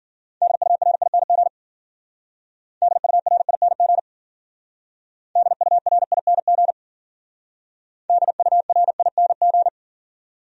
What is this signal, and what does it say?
Morse code, heard as a keyed tone, about 48 wpm, 700 Hz. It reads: DURING DURING DURING DURING
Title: Speed-Racing Top 500 Words 40wpm